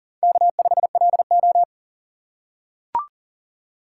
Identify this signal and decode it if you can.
Morse code, heard as a keyed tone, about 40 wpm, 700 Hz. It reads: K5LO
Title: USA Call Signs - 4 Characters - 3x Character Spacing - 40wpm